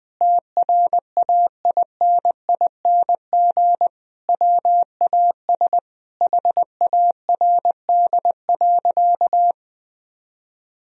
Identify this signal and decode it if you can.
Morse code, heard as a keyed tone, about 20 wpm, 700 Hz. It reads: TRAINING WAS HARD.
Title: Sentences from Top 1000 Words - Review Part 4 - 20wpm